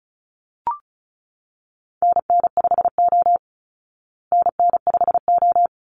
Morse code, heard as a keyed tone, about 34 wpm, 700 Hz. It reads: NN5O NN5O
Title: Call Signs Repeated 3x - 35wpm